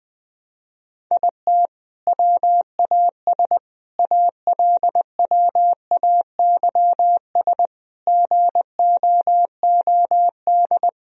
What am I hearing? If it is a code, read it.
Morse code, heard as a keyed tone, about 20 wpm, 700 Hz. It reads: IT WAS ALWAYS GOOD
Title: Sentences from Top 1000 Words - Review Part 4 - 20wpm